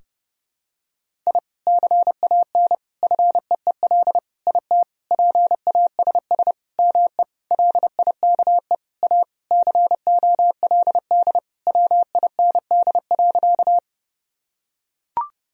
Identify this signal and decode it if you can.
Morse code, heard as a keyed tone, about 32 wpm, 700 Hz. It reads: I CAN FEEL IT PASS ME LIKE A COLD WIND.
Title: Sentences from Top 1000 Words - Review Part 2 - 30wpm